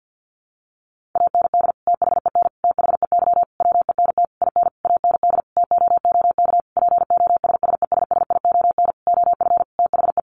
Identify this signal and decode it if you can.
Morse code, heard as a keyed tone, about 50 wpm, 700 Hz. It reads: AND THEN THEY WENT IN AND TOOK POSSESSION OF THE
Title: Sentences from Top 1400 Words 50wpm